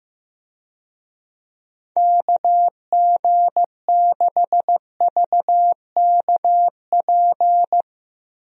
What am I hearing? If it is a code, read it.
Morse code, heard as a keyed tone, about 15 wpm, 700 Hz. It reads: KG6VKP
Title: USA Call Signs - 6 Characters - 15wpm